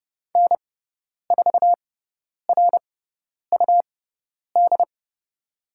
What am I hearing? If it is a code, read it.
Morse code, heard as a keyed tone, about 30 wpm, spaced as for 10 wpm, 700 Hz. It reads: N4RUD
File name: Lesson 18g - Letter m - Call Signs 30wpm